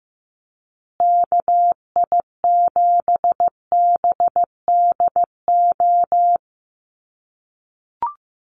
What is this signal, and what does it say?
Morse code, heard as a keyed tone, about 15 wpm, 700 Hz. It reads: KI7BDO
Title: USA Call Signs Repeated 2x 15wpm